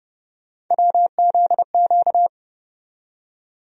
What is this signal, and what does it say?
Morse code, heard as a keyed tone, about 30 wpm, 700 Hz. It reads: WZQ
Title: USA Call Sign Postfixes - 3 Letters 30wpm